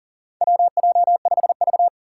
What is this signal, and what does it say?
Morse code, heard as a keyed tone, about 38 wpm, 700 Hz. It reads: WJ5V
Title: Four Challenging Letter-Number Combinations 40wpm